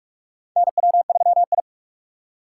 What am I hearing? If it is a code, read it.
Morse code, heard as a keyed tone, about 46 wpm, 700 Hz. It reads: NW3I
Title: USA Call Signs - 4 Characters - 6x Character Spacing - 45wpm